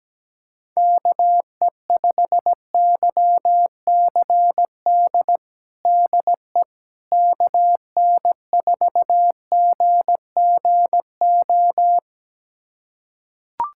Morse code, heard as a keyed tone, about 17 wpm, 700 Hz. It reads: KE5YCD DE KN4GGO
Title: USA Callsign DE Callsign - 6 Characters - 17wpm